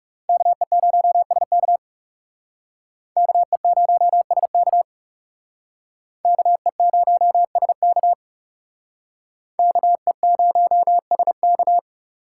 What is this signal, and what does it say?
Morse code, heard as a keyed tone, about 39 wpm, 700 Hz. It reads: KE0SK KE0SK KE0SK KE0SK